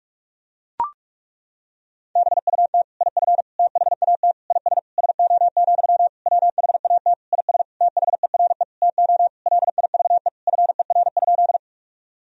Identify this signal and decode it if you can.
Morse code, heard as a keyed tone, about 41 wpm, 700 Hz. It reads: BUT IF THAT IS SO, WHAT IS THERE TO LIVE FER?
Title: Sentences from Top 200 Words and Abbreviations 45wpm